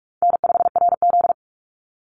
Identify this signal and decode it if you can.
Morse code, heard as a keyed tone, about 44 wpm, 700 Hz. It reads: N5RZ
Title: Easy Call Signs Repeated 2x - 45wpm